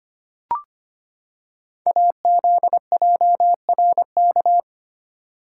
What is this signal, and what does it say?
Morse code, heard as a keyed tone, about 25 wpm, 700 Hz. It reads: AZJRK
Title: Code Groups - 5 Letter:Numbers - 25wpm